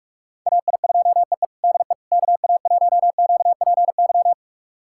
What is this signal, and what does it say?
Morse code, heard as a keyed tone, about 41 wpm, 700 Hz. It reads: AI2EE DE KA1QPY
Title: USA Callsign DE Callsign 45wpm